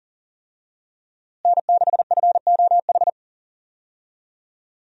Morse code, heard as a keyed tone, about 37 wpm, 700 Hz. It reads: N6FOH
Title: USA Call Signs - 5 Characters - Repeated 3x - 40wpm